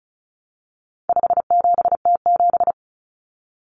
Morse code, heard as a keyed tone, about 34 wpm, 700 Hz. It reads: 57T7
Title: Common Cut Numbers - 4-Digit 35wpm